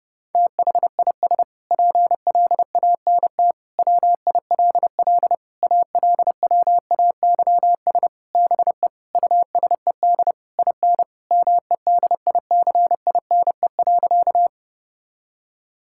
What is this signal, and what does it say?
Morse code, heard as a keyed tone, about 30 wpm, 700 Hz. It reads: THIS PLANT WILL ALWAYS BE USED IN MEDICINE.
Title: Sentences from Top 1000 Words - Review Part 6 - 30wpm